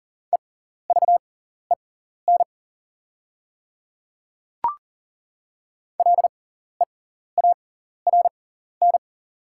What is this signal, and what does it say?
Morse code, heard as a keyed tone, about 40 wpm, spaced as for 14 wpm, 700 Hz. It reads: EVEN LEARN
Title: 6x Character Spacing - Top 500 Words - 40wpm